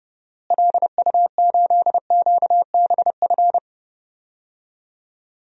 Morse code, heard as a keyed tone, about 31 wpm, 700 Hz. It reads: LU8QBF
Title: DXCC-100 Call Signs - 6 Characters - Repeated 2x - 30wpm